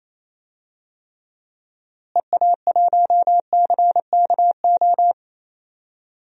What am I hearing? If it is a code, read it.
Morse code, heard as a keyed tone, about 28 wpm, 700 Hz. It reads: EA1CKO
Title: DXCC-100 Call Signs - 6 Characters - Repeated 2x - 28wpm